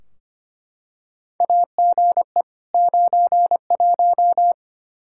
Morse code, heard as a keyed tone, about 25 wpm, 700 Hz. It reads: AGE 91
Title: Practice QSO - Exchange 3 - More Info - 25wpm